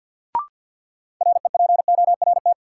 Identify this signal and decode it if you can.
Morse code, heard as a keyed tone, about 52 wpm, 700 Hz. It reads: REPORT
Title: Sentences from Top 900 Words - New Word Drill - 50wpm